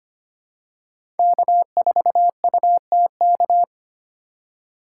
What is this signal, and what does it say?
Morse code, heard as a keyed tone, about 25 wpm, 700 Hz. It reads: K4UTK
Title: USA Call Signs - 5 Characters - 2x Character Spacing - 25wpm